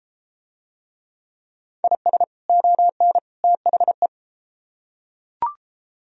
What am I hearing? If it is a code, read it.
Morse code, heard as a keyed tone, about 31 wpm, 700 Hz. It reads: IS ON THE